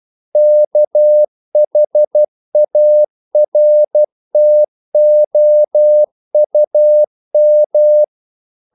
Morse code, heard as a keyed tone, about 12 wpm, 600 Hz. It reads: KHARTOUM
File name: CW / Practice / NationalCapitals12